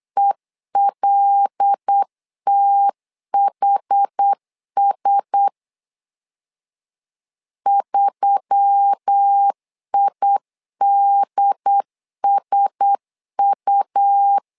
Morse code, heard as a keyed tone, about 8 wpm, 800 Hz. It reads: ELTHS 3IDSU
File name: LEZIONI / 5 LBVC / 5 L 8wpm 2831